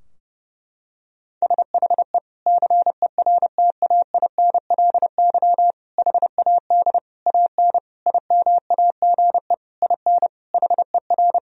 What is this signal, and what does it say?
Morse code, heard as a keyed tone, about 30 wpm, 700 Hz. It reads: SHE CERTAINLY HAD AN IMAGE IN HER
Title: Sentences from Top 1000 Words - Review Part 4 - 30wpm